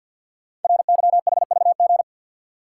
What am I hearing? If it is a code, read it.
Morse code, heard as a keyed tone, about 51 wpm, 700 Hz. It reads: RYHVG
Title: Code Groups - 5 Letters - 50wpm